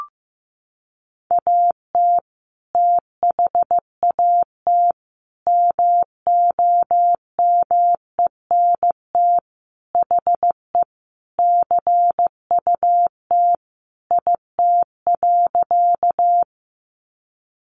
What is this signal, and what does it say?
Morse code, heard as a keyed tone, about 15 wpm, 700 Hz. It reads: AT THAT MOMENT HE CUT IT.
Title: Sentences from Top 1000 Words 15wpm